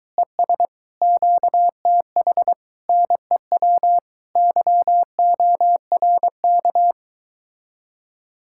Morse code, heard as a keyed tone, about 22 wpm, 700 Hz. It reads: ES QTH NEW YORK